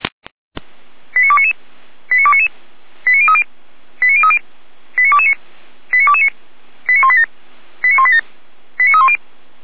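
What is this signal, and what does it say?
Five-tone selective call (ZVEI), heard as quick runs of five tones, tones of about 70 ms. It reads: 89299 89299 89030 89030 89229 89229 89187 89187 89310
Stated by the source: ZVEI five-tone call received by SDR.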